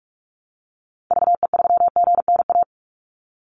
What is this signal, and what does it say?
Morse code, heard as a keyed tone, about 44 wpm, 700 Hz. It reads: VE3GNA
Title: Call Signs - 45wpm